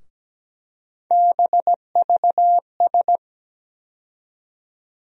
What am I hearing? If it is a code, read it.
Morse code, heard as a keyed tone, about 17 wpm, 700 Hz. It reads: BVS
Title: USA Call Sign Postfixes - 3 Letters 17wpm